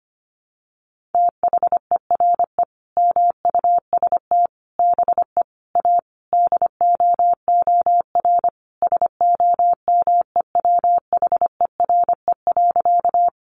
Morse code, heard as a keyed tone, about 25 wpm, 700 Hz. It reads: THERE MUST BE A DOOR SOMEWHERE.
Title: Sentences from Top 1400 Words 25wpm